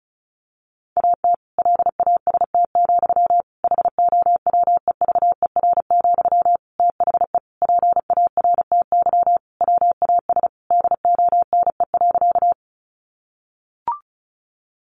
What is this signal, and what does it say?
Morse code, heard as a keyed tone, about 34 wpm, 700 Hz. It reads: AT LAST, HOWEVER, THE PARTY WAS DONE.
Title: Sentences from Top 1400 Words 35wpm